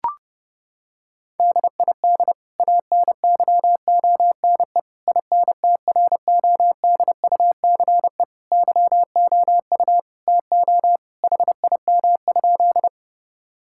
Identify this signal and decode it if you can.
Morse code, heard as a keyed tone, about 30 wpm, 700 Hz. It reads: DID ANYONE INTRODUCE YOU TO HIM?
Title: Sentences from Top 1600 Words 30wpm